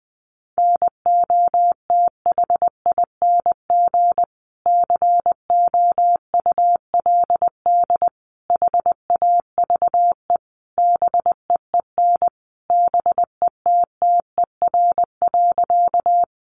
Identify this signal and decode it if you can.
Morse code, heard as a keyed tone, about 20 wpm, 700 Hz. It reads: NOTHING COULD HAVE BEEN BETTER.
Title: Sentences from Top 500 Words 20wpm